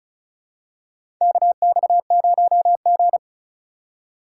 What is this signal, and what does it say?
Morse code, heard as a keyed tone, about 36 wpm, 700 Hz. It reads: KX0G
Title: USA Call Signs - 2x Character Spacing - 35wpm